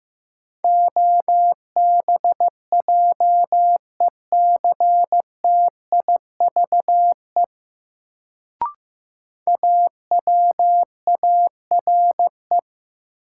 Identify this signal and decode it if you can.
Morse code, heard as a keyed tone, about 15 wpm, 700 Hz. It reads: OBJECTIVE AWARE
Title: Sentences from Top 1200 Words - New Word Drill - 15wpm